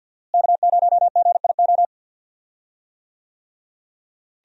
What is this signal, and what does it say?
Morse code, heard as a keyed tone, about 49 wpm, 700 Hz. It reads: K0GIO
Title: USA Call Signs - 5 Characters - Repeated 3x - 50wpm